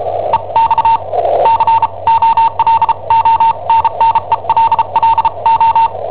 Morse code, heard as a keyed tone, about 34 wpm, spaced as for 28 wpm, 910 Hz. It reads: EX COLONNELLO
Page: Cassetta esercizi preparata da I7OHP per IW5ELP